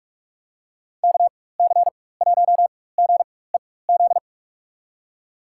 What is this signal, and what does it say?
Morse code, heard as a keyed tone, about 43 wpm, spaced as for 21 wpm, 700 Hz. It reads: KC1GEZ
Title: USA Call Signs - 4x Character Spacing - 45wpm